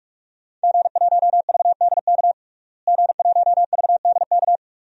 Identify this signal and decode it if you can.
Morse code, heard as a keyed tone, about 46 wpm, 700 Hz. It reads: G1VDK G1VDK
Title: DXCC-100 Call Signs - 5 Characters - Repeated 3x - 45wpm